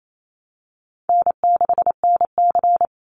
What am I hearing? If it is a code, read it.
Morse code, heard as a keyed tone, about 28 wpm, 700 Hz. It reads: N6NC
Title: Easy Call Signs Repeated 2x - 28wpm